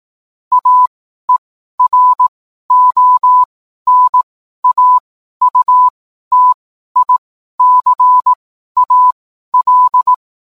Morse code, heard as a keyed tone, about 18 wpm, spaced as for 12 wpm, 1000 Hz. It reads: AERONAUTICAL
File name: cwblog / cw mp3 directory / mem morse 12